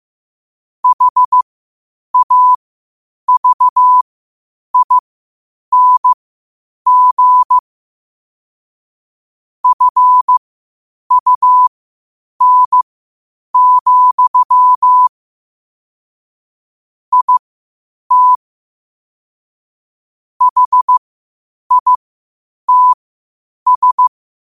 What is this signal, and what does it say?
Morse code, heard as a keyed tone, about 15 wpm, spaced as for 8 wpm, 1000 Hz. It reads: HAVING FUN, IT HITS
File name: qotdfiles / 8wpm